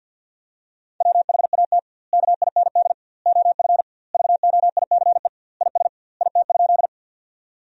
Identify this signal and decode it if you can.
Morse code, heard as a keyed tone, about 53 wpm, 700 Hz. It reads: WHAT KIND OF VOICE IS IT?
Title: 2x Word-Spacing - Sentences from Top 500 Words - 50wpm